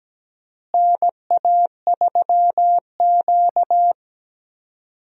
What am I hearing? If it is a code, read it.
Morse code, heard as a keyed tone, about 17 wpm, 700 Hz. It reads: NA3Q
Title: USA Call Signs - 4 Characters - Repeated 3x - 17wpm